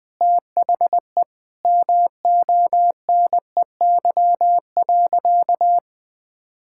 Morse code, heard as a keyed tone, about 20 wpm, 700 Hz. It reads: THE MONEY.
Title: Sentences from Top 1000 Words - Review Part 5 - 20wpm